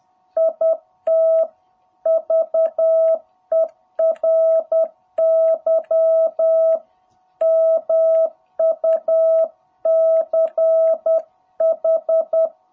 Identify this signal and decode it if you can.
Morse code, heard as a keyed tone, about 10 wpm, 660 Hz. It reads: IT VERY MUCH
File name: jh8cbh / 2021cw